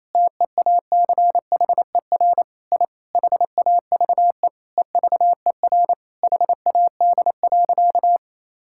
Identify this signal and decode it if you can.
Morse code, heard as a keyed tone, about 27 wpm, 700 Hz. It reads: TEACHER I HAVE EVER HAD.
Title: Sentences from Top 1000 Words - Review Part 6 - 28wpm